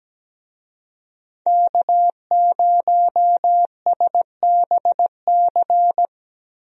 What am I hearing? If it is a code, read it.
Morse code, heard as a keyed tone, about 17 wpm, 700 Hz. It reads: K0SBC
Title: USA Call Signs - 5 Characters - Repeated 3x - 17wpm